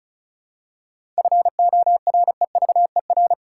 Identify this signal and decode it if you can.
Morse code, heard as a keyed tone, about 34 wpm, 700 Hz. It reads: FOREVER
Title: Seven Letter Words - 35wpm